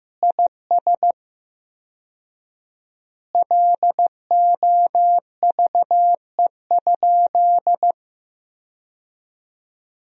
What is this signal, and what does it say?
Morse code, heard as a keyed tone, about 16 wpm, 700 Hz. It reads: IS LOVE?
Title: 4x Word-Spacing - Sentences from Top 400 Words - 15wpm